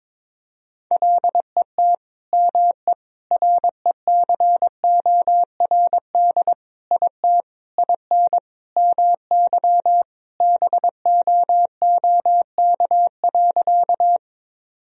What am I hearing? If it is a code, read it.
Morse code, heard as a keyed tone, about 22 wpm, 700 Hz. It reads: LET ME RECORD IT IN MY BOOK.
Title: Sentences from Top 1000 Words - Review Part 3 - 22wpm